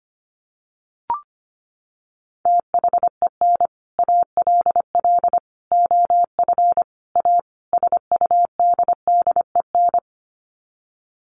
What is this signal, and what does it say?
Morse code, heard as a keyed tone, about 25 wpm, 700 Hz. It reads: THEN ALL OF A SUDDEN
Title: Sets of 5 Words Encore 25wpm